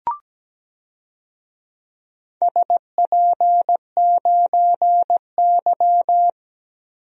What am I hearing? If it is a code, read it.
Morse code, heard as a keyed tone, about 17 wpm, 700 Hz. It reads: SP9Y
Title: DXCC-100 Call Signs - 4 Characters - Repeated 2x - 17wpm